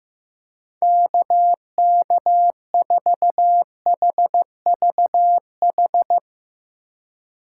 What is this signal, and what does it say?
Morse code, heard as a keyed tone, about 15 wpm, 700 Hz. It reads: KK4HVH
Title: USA Call Signs - 6 Characters - 2x Character Spacing - 15wpm